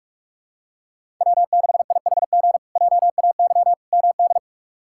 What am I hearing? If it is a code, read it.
Morse code, heard as a keyed tone, about 46 wpm, 700 Hz. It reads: W6IHG JAY MD
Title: K1USN SST QSO Practice - 45wpm